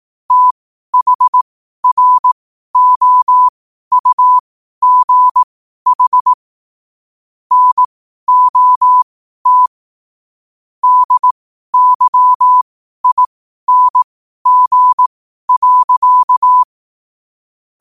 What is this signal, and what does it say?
Morse code, heard as a keyed tone, about 18 wpm, spaced as for 12 wpm, 1000 Hz. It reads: THROUGH NOT DYING.
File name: qotdfiles / 12wpm